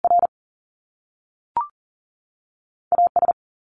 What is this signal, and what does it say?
Morse code, heard as a keyed tone, about 42 wpm, 700 Hz. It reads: R AS
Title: Top Two Letter Combinations in Words 40wpm